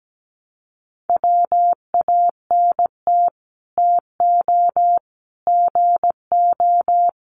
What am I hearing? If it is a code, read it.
Morse code, heard as a keyed tone, about 17 wpm, 700 Hz. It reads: WANT TO GO
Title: Sets of 3 Words 17wpm